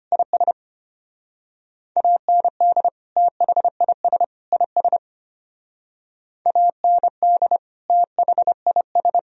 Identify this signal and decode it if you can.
Morse code, heard as a keyed tone, about 30 wpm, 700 Hz. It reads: IS AND THIS IS AND THIS
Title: Speed-Racing - Sets of 3 Words 25wpm